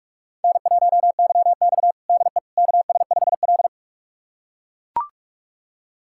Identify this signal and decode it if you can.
Morse code, heard as a keyed tone, about 42 wpm, 700 Hz. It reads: N1YX DE KS5L
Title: USA Callsign DE Callsign - 4 Characters - 45wpm